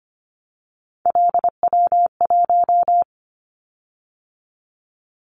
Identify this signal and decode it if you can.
Morse code, heard as a keyed tone, about 25 wpm, 700 Hz. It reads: LW1
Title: Callsign Prefixes for Top 100 DXCC - 25wpm